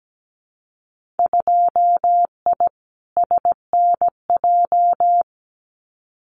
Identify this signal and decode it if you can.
Morse code, heard as a keyed tone, about 17 wpm, 700 Hz. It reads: 2I SNJ
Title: Winter Field Day Exchanges Encore 17wpm